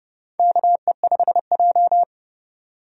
Morse code, heard as a keyed tone, about 31 wpm, 700 Hz. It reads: KE5J
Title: USA Call Signs - 4 Characters - 30wpm